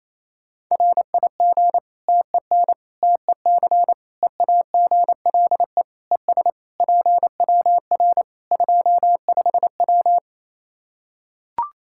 Morse code, heard as a keyed tone, about 27 wpm, 700 Hz. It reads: RIG TEN TEC EAGLE ES PWR 25W
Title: K1USN SST QSO Practice 28wpm